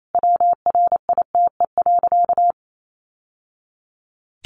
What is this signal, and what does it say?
Morse code, heard as a keyed tone, about 28 wpm, 700 Hz. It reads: WRITE.
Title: Sentences from Top 800 Words 28wpm